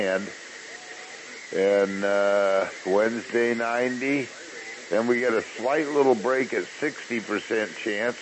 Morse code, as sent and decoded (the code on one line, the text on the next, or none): none